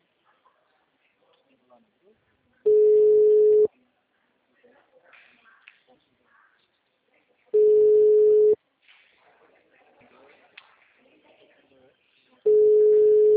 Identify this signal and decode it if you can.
Ringback tone (Europe, 425 Hz), with speech.